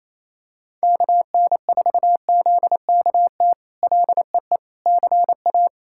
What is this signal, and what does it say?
Morse code, heard as a keyed tone, about 28 wpm, 700 Hz. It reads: KN4ZKT LEE CA
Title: K1USN SST QSO Practice - 28wpm